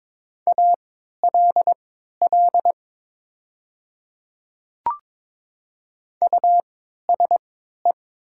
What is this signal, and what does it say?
Morse code, heard as a keyed tone, about 22 wpm, spaced as for 12 wpm, 700 Hz. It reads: ALL USE
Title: 3x Character Spacing - Top 300 Words - 22wpm